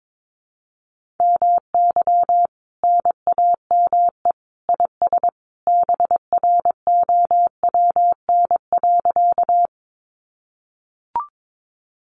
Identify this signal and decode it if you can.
Morse code, heard as a keyed tone, about 22 wpm, 700 Hz. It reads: MY NAME IS BROWN.
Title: Sentences from Top 1200 Words 22wpm